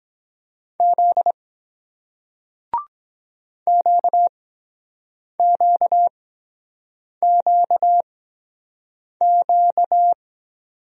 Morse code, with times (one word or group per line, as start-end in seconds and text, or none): 0.8-1.3 Z
3.7-10.1 QQQQ